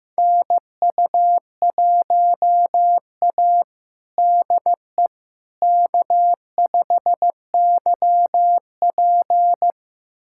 Morse code, heard as a keyed tone, about 15 wpm, 700 Hz. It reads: NU1A DE K5YP
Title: USA Callsign DE Callsign - 4 Characters - 15wpm